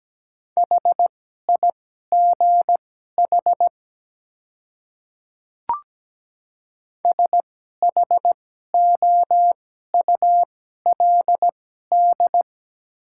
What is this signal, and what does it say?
Morse code, heard as a keyed tone, about 17 wpm, spaced as for 12 wpm, 700 Hz. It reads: HIGH SHOULD
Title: 2x Character Spacing - Top 300 Words - 17wpm